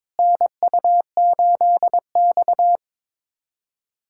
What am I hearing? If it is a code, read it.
Morse code, heard as a keyed tone, about 21 wpm, 700 Hz. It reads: NU8X
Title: USA Callsign DE Callsign - 4 Characters - 22wpm